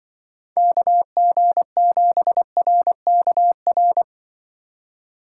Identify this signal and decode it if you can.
Morse code, heard as a keyed tone, about 24 wpm, 700 Hz. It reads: KG7RKR
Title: Speed Racing - USA 2x3 Call Signs 20wpm